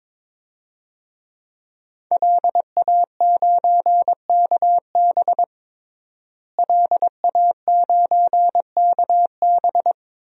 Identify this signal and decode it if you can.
Morse code, heard as a keyed tone, about 21 wpm, 700 Hz. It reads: LA9KB LA9KB
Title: DXCC-100 Call Signs - 5 Characters - Repeated 3x - 22wpm